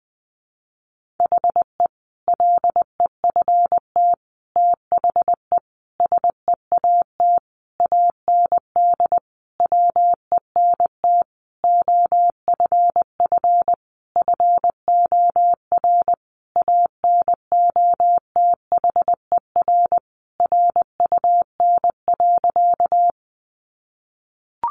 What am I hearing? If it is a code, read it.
Morse code, heard as a keyed tone, about 20 wpm, 700 Hz. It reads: HE LEFT THE SEAT AND WENT OFF FOR ANOTHER RUN.
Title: Sentences from Top 1500 Words 20wpm